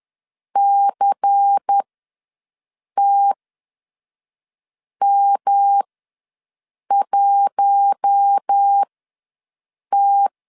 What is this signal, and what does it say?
Morse code, heard as a keyed tone, about 10 wpm, spaced as for 5 wpm, 800 Hz. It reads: CTM1T
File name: LEZIONI / 7 GF9Z / 7 GF ws 10wpm 3918